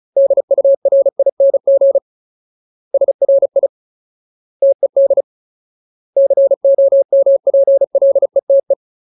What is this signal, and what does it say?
Morse code, heard as a keyed tone, about 35 wpm, 550 Hz. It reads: DURING SRI TED COMPLETE
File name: Ultimate Word List / 35WPM Ultimate